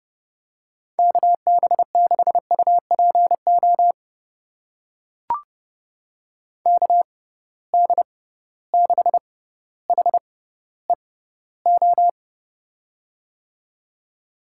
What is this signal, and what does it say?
Morse code, heard as a keyed tone, about 30 wpm, spaced as for 15 wpm, 700 Hz. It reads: KB6UPO KD6HEO
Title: USA Call Signs - 6 Characters - 6x Character Spacing - 30wpm